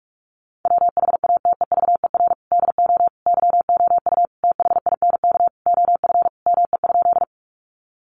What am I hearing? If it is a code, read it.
Morse code, heard as a keyed tone, about 44 wpm, 700 Hz. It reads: WHATEVER DO YOU THINK OF ME?
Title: Sentences from Top 1100 Words 45wpm